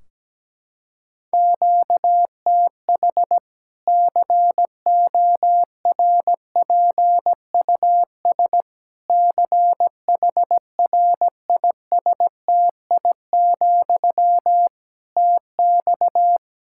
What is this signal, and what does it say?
Morse code, heard as a keyed tone, about 17 wpm, 700 Hz. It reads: QTH CORPUS CHRISTI, TX
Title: K1USN SST QSO Practice 17wpm